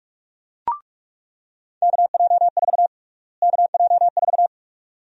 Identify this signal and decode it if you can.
Morse code, heard as a keyed tone, about 46 wpm, 700 Hz. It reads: KJ4 KJ4
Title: USA Call Sign Prefixes - Repeated 2x - 45wpm